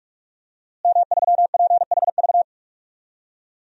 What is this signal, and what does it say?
Morse code, heard as a keyed tone, about 46 wpm, 700 Hz. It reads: M3PHV
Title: DXCC-100 Call Signs - 5 Characters - Repeated 2x - 45wpm